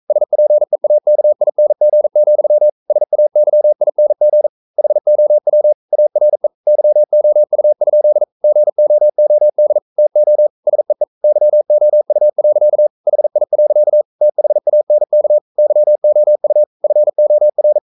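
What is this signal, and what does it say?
Morse code, heard as a keyed tone, about 42 wpm, 600 Hz. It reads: SPEAKING, SAYING HOW ARE YOU? GOOD TO SEE YOU. HI. THANK YOU FOR
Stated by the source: Morse code practice at 41 wpm, from CW med Gnister.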